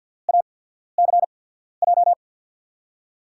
Morse code, heard as a keyed tone, about 47 wpm, spaced as for 13 wpm, 700 Hz. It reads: ACJ
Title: USA Call Signs - 6 Characters - 8x Character Spacing - 50wpm